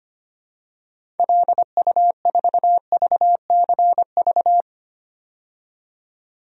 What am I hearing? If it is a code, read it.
Morse code, heard as a keyed tone, about 25 wpm, 690 Hz. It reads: LU4VCV
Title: DXCC-100 Call Signs - 6 Characters - 25wpm